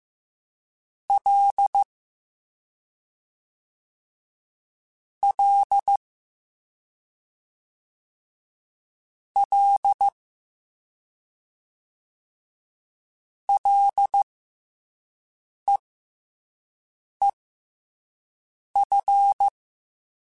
Morse code, heard as a keyed tone, about 15 wpm, spaced as for 5 wpm, 800 Hz. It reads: L L L LEEF
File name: F and L / F and L mp3 / F and L Prefixes 3x2 Twice @5